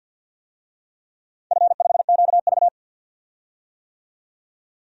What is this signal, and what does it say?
Morse code, heard as a keyed tone, about 51 wpm, 700 Hz. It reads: F5QV
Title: DXCC-100 Call Signs - 4 Characters - 50wpm